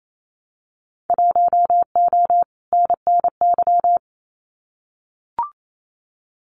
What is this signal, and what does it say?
Morse code, heard as a keyed tone, about 28 wpm, 700 Hz. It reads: 1O NNY
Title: Winter Field Day Exchanges 28wpm